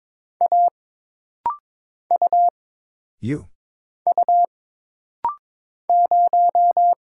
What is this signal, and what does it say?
Morse code, heard as a keyed tone, about 22 wpm, 700 Hz. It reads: A U U 0
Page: This rapid-fire timing is geared to push your ICR (Instant Character Recognition) skills.